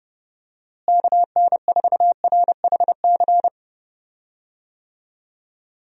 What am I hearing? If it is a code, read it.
Morse code, heard as a keyed tone, about 32 wpm, 700 Hz. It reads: KN4RHC
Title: USA Call Signs - 6 Characters - Repeated 3x - 30wpm